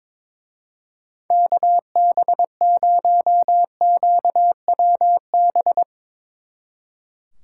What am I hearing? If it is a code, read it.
Morse code, heard as a keyed tone, about 22 wpm, 700 Hz. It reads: KB0QWB
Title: Speed Racing - USA 2x3 Call Signs 22wpm